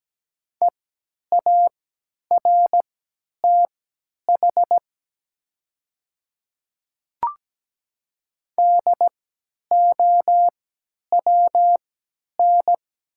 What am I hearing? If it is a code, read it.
Morse code, heard as a keyed tone, about 17 wpm, spaced as for 10 wpm, 700 Hz. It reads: EARTH DOWN
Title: 3x Character Spacing - Top 200 Words - 17wpm